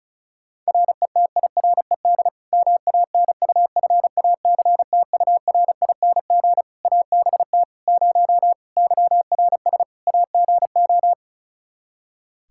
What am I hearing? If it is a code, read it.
Morse code, heard as a keyed tone, about 34 wpm, 700 Hz. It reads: RETIRED MANUFACTURING ABT 0 YRS AGO
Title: Practice QSO - Exchange 3 - More Info - 35wpm